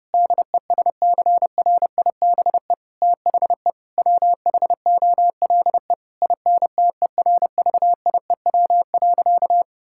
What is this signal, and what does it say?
Morse code, heard as a keyed tone, about 30 wpm, 700 Hz. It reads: DESCRIBE THE WHOLE INTERVIEW.
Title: Sentences from Top 1600 Words 30wpm